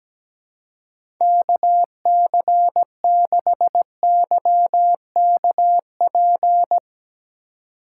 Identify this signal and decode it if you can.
Morse code, heard as a keyed tone, about 17 wpm, 700 Hz. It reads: KC6YKP
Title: USA Call Signs - 2x Character Spacing - 17wpm